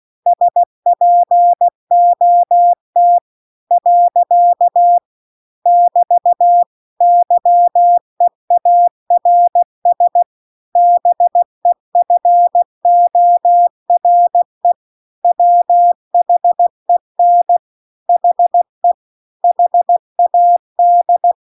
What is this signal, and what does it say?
Morse code, heard as a keyed tone, about 16 wpm, 700 Hz. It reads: SPOT . <BT> YEARS BEFORE WHEN HE HAD
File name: CW-The-Big-Twoheaded-River-2-16WPM